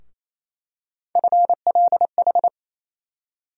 Morse code, heard as a keyed tone, about 28 wpm, 700 Hz. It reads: FLH
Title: USA Call Sign Postfixes - 3 Letters 28wpm